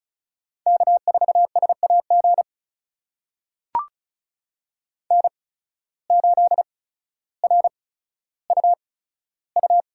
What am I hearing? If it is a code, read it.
Morse code, heard as a keyed tone, about 36 wpm, spaced as for 15 wpm, 690 Hz. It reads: K4SAG N8RUU
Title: USA Call Signs - 5 Characters - 8x Character Spacing - 35wpm